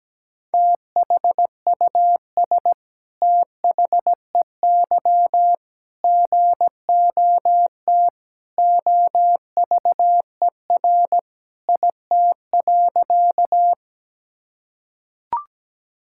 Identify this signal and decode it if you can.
Morse code, heard as a keyed tone, about 17 wpm, 700 Hz. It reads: THUS THEY GOT OVER IT.
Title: Sentences from Top 1000 Words - Review Part 5 - 17wpm